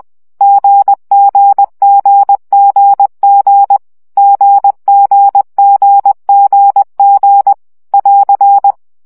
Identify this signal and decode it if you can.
Morse code, heard as a keyed tone, about 20 wpm, 800 Hz. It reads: GGGGG GGGGG <AR>
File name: CW gyakorlas g 17 baud